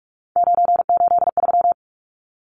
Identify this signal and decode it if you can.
Morse code, heard as a keyed tone, about 44 wpm, 700 Hz. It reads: 983
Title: Numbers - 3-Digit 45wpm